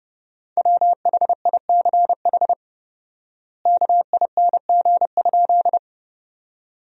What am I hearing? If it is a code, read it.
Morse code, heard as a keyed tone, about 30 wpm, 700 Hz. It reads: WHICH KING?
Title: 4x Word-Spacing - Sentences from Top 400 Words - 30wpm